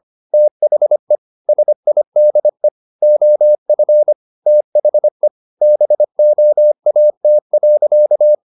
Morse code, heard as a keyed tone, about 25 wpm, 600 Hz. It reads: THE SIDE OF THE BOAT.
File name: the old man and the sea part 1 25wpm